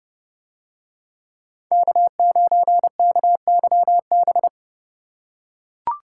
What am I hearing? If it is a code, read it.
Morse code, heard as a keyed tone, about 29 wpm, 700 Hz. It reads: K9KYB